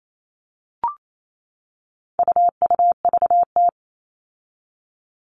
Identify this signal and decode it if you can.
Morse code, heard as a keyed tone, about 28 wpm, 700 Hz. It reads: UUVT
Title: All Cut Numbers - 4-Digit 28wpm